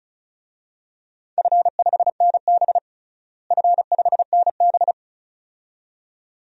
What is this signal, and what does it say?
Morse code, heard as a keyed tone, about 36 wpm, 700 Hz. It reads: F5NB F5NB
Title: DXCC-100 Call Signs - 4 Characters - Repeated 2x - 35wpm